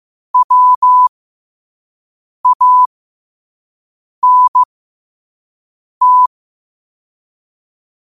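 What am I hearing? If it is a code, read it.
Morse code, heard as a keyed tone, about 15 wpm, spaced as for 5 wpm, 1000 Hz. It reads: WANT
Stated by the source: Morse quote of the day at 5 wpm.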